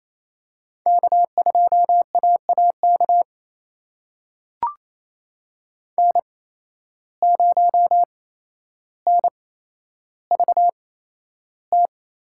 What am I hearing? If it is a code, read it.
Morse code, heard as a keyed tone, about 28 wpm, spaced as for 12 wpm, 700 Hz. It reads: K2AAK N0NVT
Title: USA Call Signs - 5 Characters - 8x Character Spacing - 28wpm